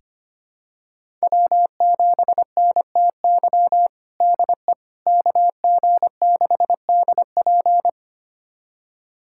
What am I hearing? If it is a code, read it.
Morse code, heard as a keyed tone, about 25 wpm, 700 Hz. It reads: W7NTY DE KG6DP
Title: USA Callsign DE Callsign - 5 Characters - 25wpm